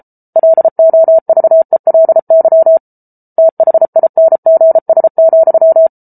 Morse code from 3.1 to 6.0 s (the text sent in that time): THINGS,